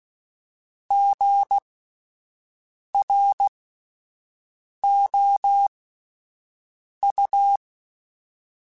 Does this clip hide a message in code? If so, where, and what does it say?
Yes: Morse code from 0.9 to 7.6 s, heard as a keyed tone, about 16 wpm, spaced as for 5 wpm, 800 Hz. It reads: GROU